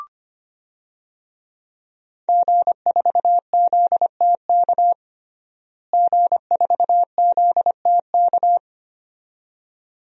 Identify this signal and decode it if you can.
Morse code, heard as a keyed tone, about 25 wpm, 700 Hz. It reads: G4ZTK G4ZTK
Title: DXCC-100 Call Signs - 5 Characters - Repeated 2x - 25wpm